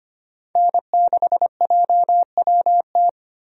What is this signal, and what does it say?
Morse code, heard as a keyed tone, about 25 wpm, 700 Hz. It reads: N6JWT
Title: USA Call Signs - 5 Characters - Repeated 2x - 25wpm